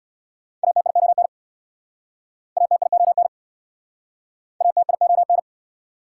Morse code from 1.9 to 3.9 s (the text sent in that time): ATION